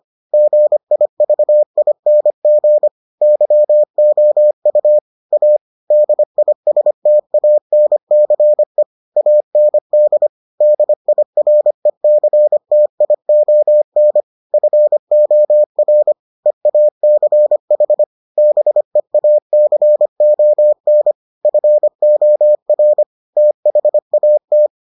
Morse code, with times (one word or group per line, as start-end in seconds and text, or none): 0.3-2.9 GIVING
3.2-5.0 YOU
5.3-5.6 A
5.9-8.8 DISTANCE
9.2-10.3 AND
10.6-14.2 DIRECTION
14.5-16.1 FOR
16.5-18.0 EACH
18.4-21.1 BEACON
21.4-23.0 FOR
23.4-24.7 THAT